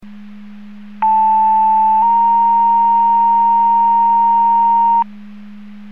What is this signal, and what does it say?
Two-tone page, heard as a one-second tone, then a three-second tone, about 880 then 930 Hz. It is heard from 1.0 to 5.0 s.